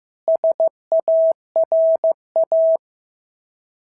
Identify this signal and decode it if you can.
Morse code, heard as a keyed tone, about 14 wpm, 650 Hz. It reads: SARA
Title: SOTA QSO Practice 15wpm